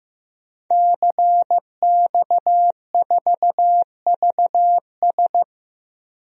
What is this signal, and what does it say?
Morse code, heard as a keyed tone, about 15 wpm, 700 Hz. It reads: CX4VS
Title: DXCC-100 Call Signs - 5 Characters - Repeated 3x - 15wpm